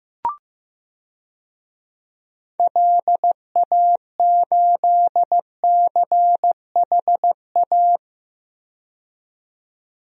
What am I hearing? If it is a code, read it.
Morse code, heard as a keyed tone, about 15 wpm, 700 Hz. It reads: LA8CHA
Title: DXCC-100 Call Signs - 6 Characters - 15wpm